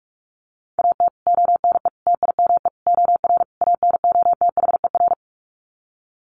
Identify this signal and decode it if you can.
Morse code, heard as a keyed tone, about 44 wpm, 700 Hz. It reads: AT ONE TIME OR ANOTHER
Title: Sets of 5 Words Encore 45wpm